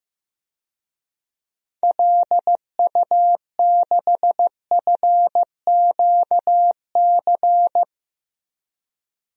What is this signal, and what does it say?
Morse code, heard as a keyed tone, about 15 wpm, 700 Hz. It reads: LU6FQC
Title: DXCC-100 Call Signs - 6 Characters - 15wpm